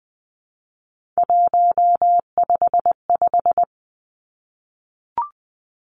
Morse code, heard as a keyed tone, about 20 wpm, 700 Hz. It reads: 155
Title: Three Challenging Letter-Number Combinations 20wpm